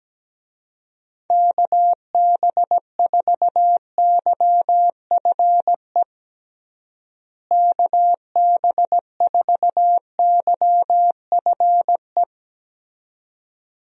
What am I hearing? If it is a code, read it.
Morse code, heard as a keyed tone, about 17 wpm, 700 Hz. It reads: KB4YFE KB4YFE
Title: USA Call Signs - 6 Characters - Repeated 2x - 17wpm